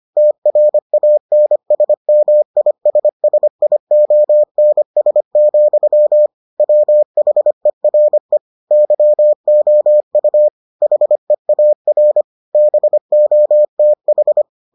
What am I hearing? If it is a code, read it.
Morse code, heard as a keyed tone, about 25 wpm, 600 Hz. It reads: TRANSMISSIONS, WHERE YOU HEAR BOTH